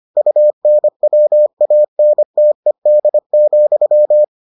Morse code, heard as a keyed tone, about 25 wpm, 600 Hz. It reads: UNWANTED,